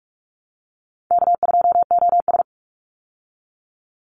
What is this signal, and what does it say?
Morse code, heard as a keyed tone, about 44 wpm, 700 Hz. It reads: K2OS
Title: Easy Call Signs Repeated 2x - 45wpm